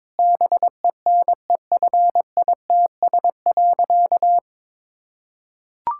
Morse code, heard as a keyed tone, about 22 wpm, 700 Hz. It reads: BENEFITS.
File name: Difficult Sentences with Hints - 22wpm